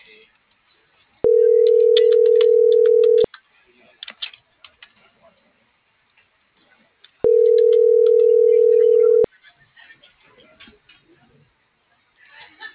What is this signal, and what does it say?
Ringback tone (North America), 440 + 480 Hz.